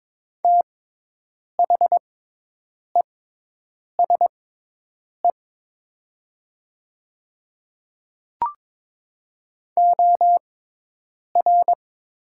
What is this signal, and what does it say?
Morse code, heard as a keyed tone, about 22 wpm, spaced as for 8 wpm, 700 Hz. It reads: THESE OR
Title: 6x Character Spacing - Top 300 Words - 22wpm